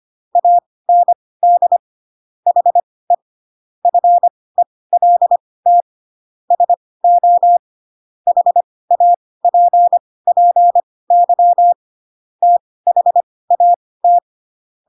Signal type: Morse code